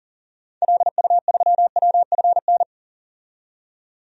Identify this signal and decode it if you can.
Morse code, heard as a keyed tone, about 40 wpm, 700 Hz. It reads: LU3WFN
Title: DXCC-100 Call Signs - 6 Characters - 40wpm